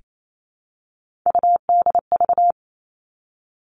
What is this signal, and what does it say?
Morse code, heard as a keyed tone, about 28 wpm, 700 Hz. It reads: UDV
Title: All Cut Numbers - 3-Digit 28wpm